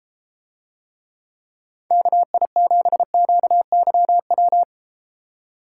Morse code, heard as a keyed tone, about 32 wpm, 700 Hz. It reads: KI7QYW